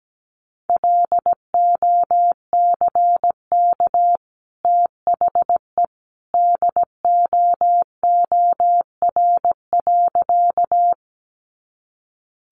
Morse code, heard as a keyed tone, about 17 wpm, 700 Hz. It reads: LOCK THE DOOR.
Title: Sentences from Top 1300 Words 17wpm